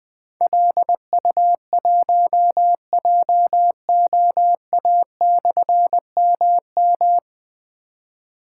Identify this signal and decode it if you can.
Morse code, heard as a keyed tone, about 21 wpm, 700 Hz. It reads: LU1JOA/MM
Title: Difficult Call Signs 20wpm